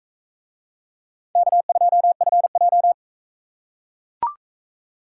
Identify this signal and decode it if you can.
Morse code, heard as a keyed tone, about 40 wpm, 700 Hz. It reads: K2FJ